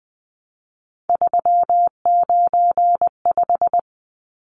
Morse code, heard as a keyed tone, about 20 wpm, 700 Hz. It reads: 395
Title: Numbers - 3-Digit 20wpm